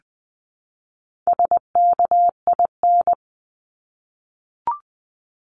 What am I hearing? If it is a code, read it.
Morse code, heard as a keyed tone, about 20 wpm, 700 Hz. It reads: SKIN
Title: Top 1001-1250 Words 20wpm